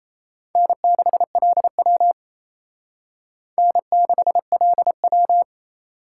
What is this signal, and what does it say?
Morse code, heard as a keyed tone, about 34 wpm, 700 Hz. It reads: N6LW N6LW
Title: Speed Racing - USA 1x2 Call Signs 28wpm